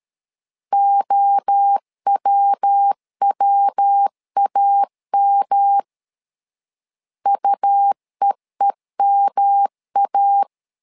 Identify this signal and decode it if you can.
Morse code, heard as a keyed tone, about 13 wpm, 800 Hz. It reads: OWWAM UEEMA
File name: LEZIONI / 3 MWRO / 3 MWRO 12wpm 1730